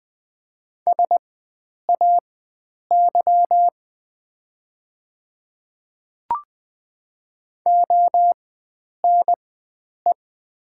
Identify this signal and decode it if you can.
Morse code, heard as a keyed tone, about 20 wpm, spaced as for 9 wpm, 700 Hz. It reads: SAY ONE
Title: 4x Character Spacing - Top 300 Words - 20wpm